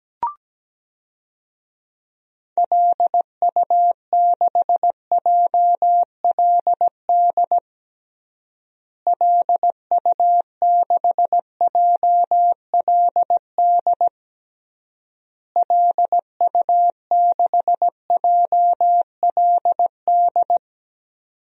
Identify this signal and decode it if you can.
Morse code, heard as a keyed tone, about 17 wpm, 700 Hz. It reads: LU6JLD LU6JLD LU6JLD
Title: DXCC-100 Call Signs - 6 Characters - Repeated 3x - 17wpm